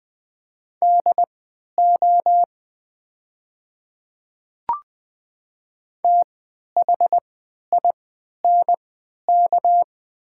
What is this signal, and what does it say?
Morse code, heard as a keyed tone, about 20 wpm, spaced as for 11 wpm, 700 Hz. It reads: DO THINK
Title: 3x Character Spacing - Top 200 Words - 20wpm